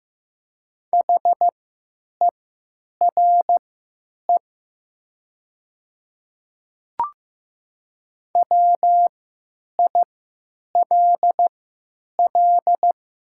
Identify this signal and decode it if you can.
Morse code, heard as a keyed tone, about 15 wpm, spaced as for 9 wpm, 700 Hz. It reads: HERE WILL
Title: 3x Character Spacing - Top 500 Words - 15wpm